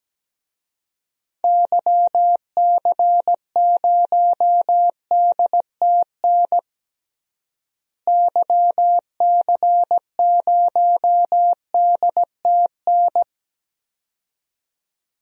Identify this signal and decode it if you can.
Morse code, heard as a keyed tone, about 17 wpm, 700 Hz. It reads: YC0DTN YC0DTN
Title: DXCC-100 Call Signs - 6 Characters - Repeated 2x - 17wpm